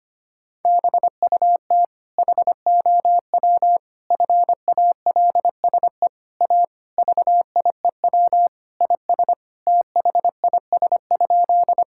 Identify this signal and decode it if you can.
Morse code, heard as a keyed tone, about 25 wpm, 700 Hz. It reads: BUT HOW FALSE A VIEW IS THIS?
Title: Sentences from Top 1600 Words 25wpm